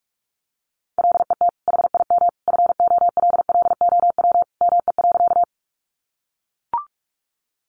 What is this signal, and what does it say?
Morse code, heard as a keyed tone, about 44 wpm, 700 Hz. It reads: LET HIM FOLLOW ME.
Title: Sentences from Top 300 Words 45wpm